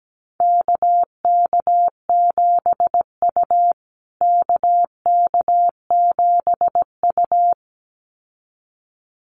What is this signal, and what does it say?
Morse code, heard as a keyed tone, about 17 wpm, 700 Hz. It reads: KK7U KK7U
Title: USA Call Signs Repeated 2x 17wpm